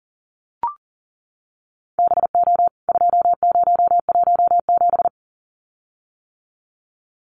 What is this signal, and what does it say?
Morse code, heard as a keyed tone, about 40 wpm, 700 Hz. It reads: BO 2017
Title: CWT Exchange 40wpm